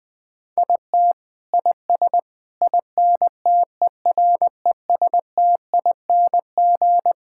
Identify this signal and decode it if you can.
Morse code, heard as a keyed tone, about 20 wpm, 700 Hz. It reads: IT IS INTERESTING
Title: Sets of 6 Words - 20wpm